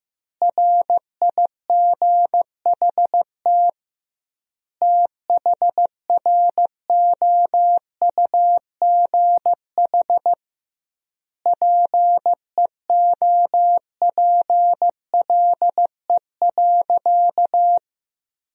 Morse code, heard as a keyed tone, about 15 wpm, 700 Hz. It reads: RIGHT THROUGH PEOPLE.
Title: 2x Word-Spacing - Sentences from Top 200 Words - 15wpm